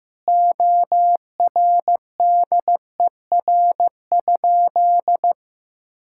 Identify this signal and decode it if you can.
Morse code, heard as a keyed tone, about 15 wpm, 700 Hz. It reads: ORDER?
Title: Sentences from Top 1600 Words 15wpm